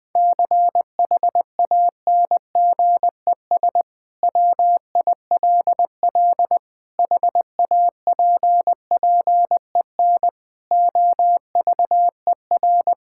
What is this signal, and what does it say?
Morse code, heard as a keyed tone, about 20 wpm, 700 Hz. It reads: CHANGES WILL HAPPEN OVER
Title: Sentences with 5-Word Phrases - 20wpm